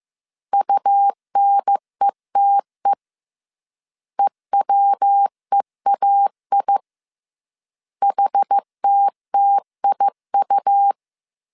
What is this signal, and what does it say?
Morse code, heard as a keyed tone, about 15 wpm, 800 Hz. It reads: UNETE EWEAI HTTIU